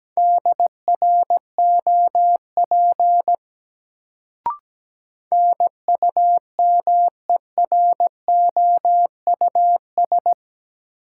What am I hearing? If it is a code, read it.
Morse code, heard as a keyed tone, about 17 wpm, 700 Hz. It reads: DROP NUMEROUS
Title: Sentences from Top 1400 Words - New Word Drill - 17wpm